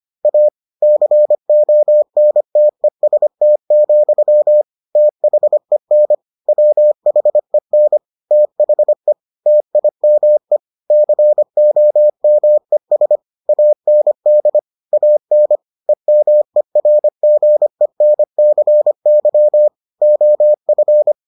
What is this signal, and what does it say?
Morse code, heard as a keyed tone, about 25 wpm, 600 Hz. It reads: A CONTEST, THEN WHEN THE TIME COMES AND AN EMERGENCY OF